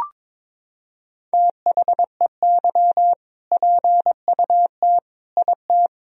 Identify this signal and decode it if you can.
Morse code, heard as a keyed tone, about 22 wpm, 700 Hz. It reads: THEY PUT IT
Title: Sentences from Top 1000 Words - Review Part 4 - 22wpm